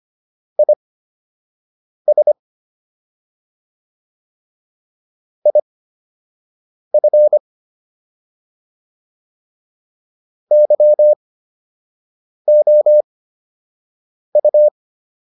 Morse code, heard as a keyed tone, about 25 wpm, spaced as for 6 wpm, 600 Hz. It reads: IS IF YOU